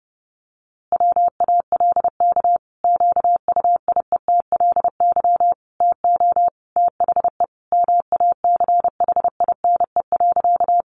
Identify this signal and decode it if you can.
Morse code, heard as a keyed tone, about 30 wpm, 700 Hz. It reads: WALK QUIETLY TO THE MACHINE.
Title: A Sound Of Thunder 30wpm